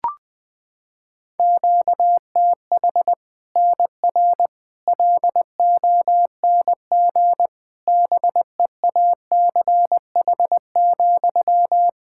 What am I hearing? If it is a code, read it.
Morse code, heard as a keyed tone, about 20 wpm, 700 Hz. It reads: QTH NR LONG BEACH,
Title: K1USN SST QSO Practice 20wpm